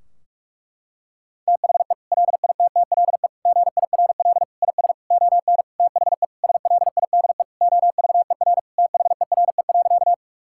Morse code, heard as a keyed tone, about 41 wpm, 700 Hz. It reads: THE LITTLE GIRL IS ON THE SLIDE OVER THERE.
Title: Sentences from Top 1600 Words 45wpm